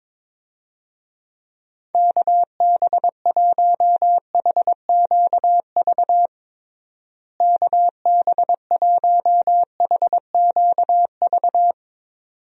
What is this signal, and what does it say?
Morse code, heard as a keyed tone, about 22 wpm, 700 Hz. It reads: KB1HQV KB1HQV
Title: USA Call Signs - 6 Characters - Repeated 2x - 22wpm